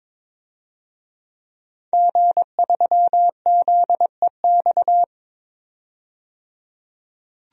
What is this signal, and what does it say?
Morse code, heard as a keyed tone, about 22 wpm, 700 Hz. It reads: G3ZEX
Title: DXCC-100 Call Signs - 5 Characters - 22wpm